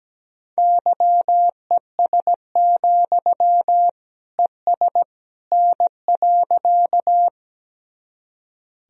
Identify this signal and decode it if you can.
Morse code, heard as a keyed tone, about 17 wpm, 700 Hz. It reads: YES, ES N.
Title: Sentences from Top 500 Words and Abbreviations 17wpm